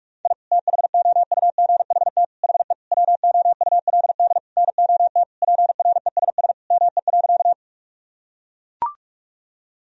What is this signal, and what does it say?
Morse code, heard as a keyed tone, about 42 wpm, 700 Hz. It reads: I THOUGHT HE WOULD NOT PRESS ME.
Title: Sentences from Top 1000 Words - Review Part 1 - 45wpm